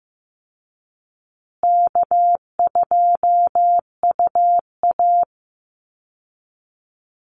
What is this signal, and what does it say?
Morse code, heard as a keyed tone, about 15 wpm, 700 Hz. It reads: K2UA
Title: Call Signs Repeated 2x - 15wpm